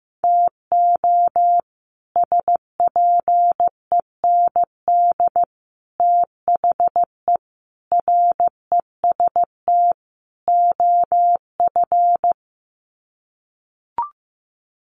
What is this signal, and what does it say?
Morse code, heard as a keyed tone, about 15 wpm, 700 Hz. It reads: TO SPEND THE REST OF
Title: Sets of 5 Words Encore2 15wpm